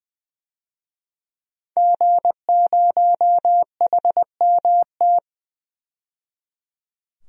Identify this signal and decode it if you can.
Morse code, heard as a keyed tone, about 20 wpm, 700 Hz. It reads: G0HMT
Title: DXCC-100 Call Signs - 5 Characters - 20wpm